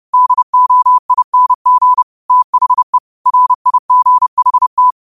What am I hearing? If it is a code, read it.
Morse code, heard as a keyed tone, about 30 wpm, 1000 Hz. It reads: DOING THE RIGHT
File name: qotdfiles / 30wpm